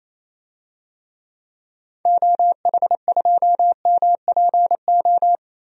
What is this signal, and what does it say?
Morse code, heard as a keyed tone, about 28 wpm, 700 Hz. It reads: OH2MPO
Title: DXCC-100 Call Signs - 6 Characters - Repeated 3x - 28wpm